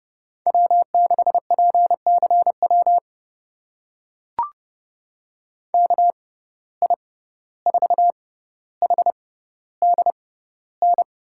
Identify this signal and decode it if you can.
Morse code, heard as a keyed tone, about 30 wpm, spaced as for 15 wpm, 700 Hz. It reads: W6PCW KI4HDN
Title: USA Call Signs - 6x Character Spacing - 30wpm